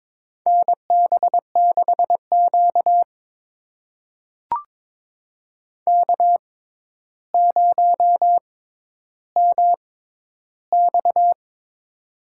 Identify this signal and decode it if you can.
Morse code, heard as a keyed tone, about 22 wpm, spaced as for 11 wpm, 700 Hz. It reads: NB6Q K0MX
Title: USA Call Signs - 4 Characters - 6x Character Spacing - 22wpm